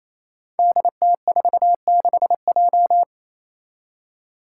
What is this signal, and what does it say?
Morse code, heard as a keyed tone, about 28 wpm, 700 Hz. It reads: DT46J
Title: Five Challenging Letter-Number Combinations 28wpm